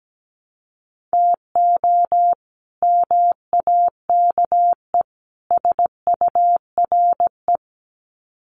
Morse code, heard as a keyed tone, about 17 wpm, 700 Hz. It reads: TO MAKE SURE
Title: Sets of 3 Words 17wpm